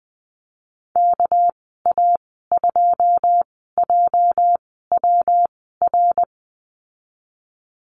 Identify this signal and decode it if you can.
Morse code, heard as a keyed tone, about 19 wpm, spaced as for 14 wpm, 700 Hz. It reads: KA2JWR
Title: 207 Lesson 33g - Letter j - Call Signs 20wpm